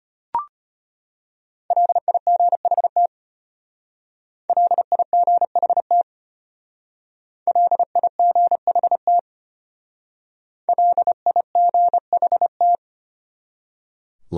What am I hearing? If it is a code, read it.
Morse code, heard as a keyed tone, about 33 wpm, 700 Hz. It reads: LIGHT LIGHT LIGHT LIGHT